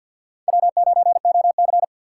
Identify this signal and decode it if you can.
Morse code, heard as a keyed tone, about 53 wpm, 700 Hz. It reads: W9OC
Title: USA Call Signs - 4 Characters - 6x Character Spacing - 50wpm